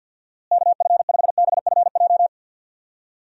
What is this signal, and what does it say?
Morse code, heard as a keyed tone, about 52 wpm, 700 Hz. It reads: KF5BFJ
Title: USA Call Signs - 6 Characters - Repeated 3x - 50wpm